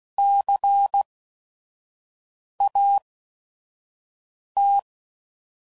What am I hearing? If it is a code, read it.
Morse code, heard as a keyed tone, about 16 wpm, spaced as for 5 wpm, 800 Hz. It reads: CAT
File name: Prefixes / By First Letter / Cs